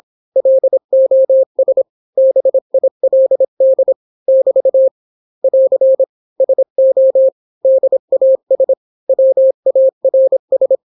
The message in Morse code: LOS BILD <BT> <AR> SO DAS WARS